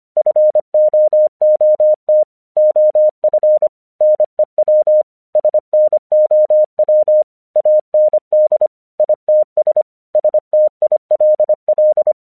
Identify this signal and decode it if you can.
Morse code, heard as a keyed tone, about 25 wpm, 610 Hz. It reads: FOOT OF NEW SNOW AND ITS STILL
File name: 1 English 25wpm / AQC02